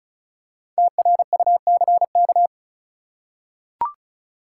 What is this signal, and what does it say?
Morse code, heard as a keyed tone, about 36 wpm, 700 Hz. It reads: TRUCK
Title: Five Letter Words - 35wpm